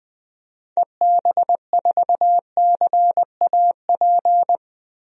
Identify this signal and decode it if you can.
Morse code, heard as a keyed tone, about 20 wpm, 700 Hz. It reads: EB4CAP
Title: DXCC-100 Call Signs - 6 Characters - 20wpm